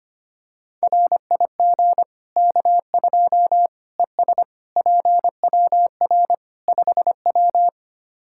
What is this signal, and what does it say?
Morse code, heard as a keyed tone, about 25 wpm, 700 Hz. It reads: RIG K2 ES PWR 5W
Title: Practice QSO - Exchange 2 - Info - 25wpm